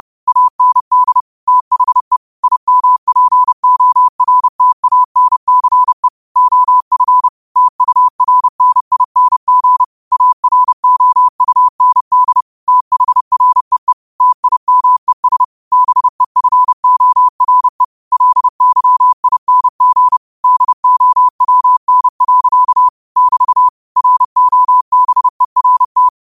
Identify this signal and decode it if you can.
Morse code, heard as a keyed tone, about 30 wpm, 1000 Hz. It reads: AND THE IMPORTANCE OF TURNING AROUND THREE TIMES BEFORE LYING DOWN. <BT> ROBERT
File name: qotdfiles / 30wpm